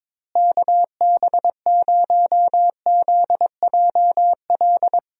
Morse code, heard as a keyed tone, about 22 wpm, 700 Hz. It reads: KB0ZJL
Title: USA Call Signs - 6 Characters - Repeated 2x - 22wpm